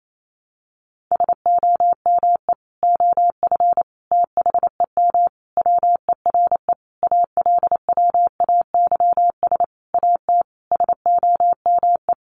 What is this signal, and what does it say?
Morse code, heard as a keyed tone, about 28 wpm, 700 Hz. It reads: SOME OF THEM WERE ALWAYS AT SOME
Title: Sentences from Top 1200 Words 28wpm